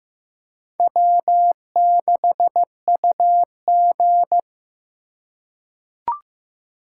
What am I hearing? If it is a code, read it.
Morse code, heard as a keyed tone, about 15 wpm, 700 Hz. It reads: W6UG
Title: USA Call Signs - 4x Character Spacing - 15wpm